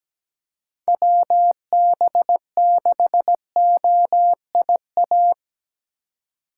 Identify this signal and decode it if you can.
Morse code, heard as a keyed tone, about 17 wpm, 700 Hz. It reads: WB6OIA
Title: USA Call Signs - 2x Character Spacing - 17wpm